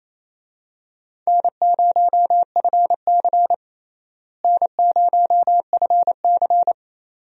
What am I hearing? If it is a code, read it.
Morse code, heard as a keyed tone, about 28 wpm, 700 Hz. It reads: N0FC N0FC
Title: USA Call Signs - 4 Characters - Repeated 2x - 28wpm